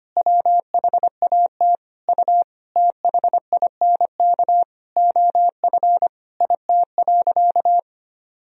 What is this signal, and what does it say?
Morse code, heard as a keyed tone, about 25 wpm, 700 Hz. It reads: WHAT U THINK OF IT.
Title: Sentences from Top 200 Words and Abbreviations 25wpm